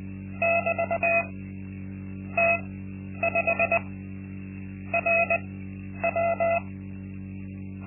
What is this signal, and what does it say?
Morse code, heard as a keyed tone, about 19 wpm, spaced as for 10 wpm, 680 Hz. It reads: <BT> T5 RW
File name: Linksmakalnis-2